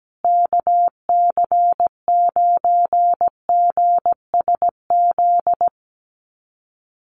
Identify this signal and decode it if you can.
Morse code, heard as a keyed tone, about 17 wpm, 700 Hz. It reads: KC9GSZ
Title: USA Call Signs Repeated 3x 17wpm